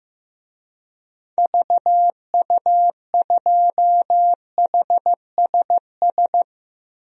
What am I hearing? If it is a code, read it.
Morse code, heard as a keyed tone, about 15 wpm, 700 Hz. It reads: VU2HSS
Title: DXCC-100 Call Signs - 6 Characters - 15wpm